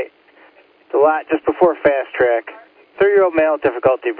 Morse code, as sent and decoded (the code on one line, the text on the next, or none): none